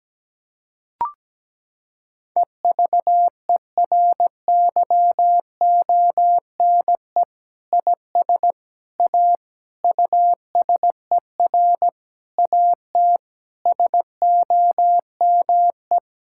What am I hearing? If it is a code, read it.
Morse code, heard as a keyed tone, about 17 wpm, 700 Hz. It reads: EVERYONE IS A USER AT SOME
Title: Sentences from Top 1000 Words - Review Part 5 - 17wpm